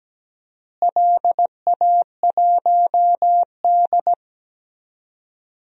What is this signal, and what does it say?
Morse code, heard as a keyed tone, about 17 wpm, 700 Hz. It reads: LA1D
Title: DXCC-100 Call Signs - 4 Characters - Repeated 3x - 17wpm